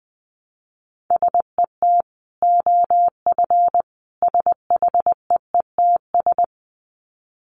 Morse code, heard as a keyed tone, about 20 wpm, 700 Hz. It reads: SET OF SHEETS
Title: Dit Challenge 20wpm